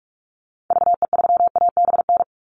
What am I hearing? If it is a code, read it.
Morse code, heard as a keyed tone, about 44 wpm, 700 Hz. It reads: VE3ABN
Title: Call Signs Repeated 2x - 45wpm